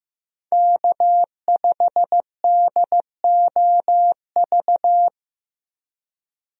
Morse code, heard as a keyed tone, about 15 wpm, 700 Hz. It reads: K5DOV
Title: USA Call Signs - 5 Characters - Repeated 3x - 15wpm